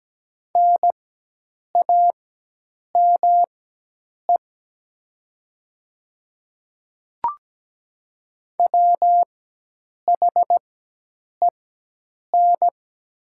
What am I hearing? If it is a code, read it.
Morse code, heard as a keyed tone, about 17 wpm, spaced as for 8 wpm, 700 Hz. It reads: NAME WHEN
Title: 4x Character Spacing - Top 300 Words - 17wpm